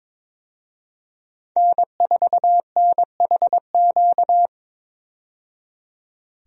Morse code, heard as a keyed tone, about 22 wpm, 700 Hz. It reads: N4NHQ
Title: USA Call Signs - 5 Characters - 22wpm